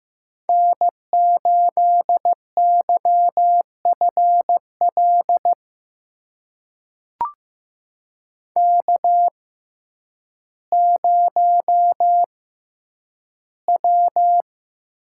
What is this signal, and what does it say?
Morse code, heard as a keyed tone, about 15 wpm, spaced as for 9 wpm, 700 Hz. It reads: N8YFL K0W
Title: USA Call Signs - 5 Characters - 6x Character Spacing - 15wpm